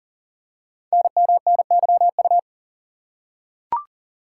Morse code, heard as a keyed tone, about 38 wpm, 690 Hz. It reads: NMNYU
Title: Code Groups - 5 Letters - 40wpm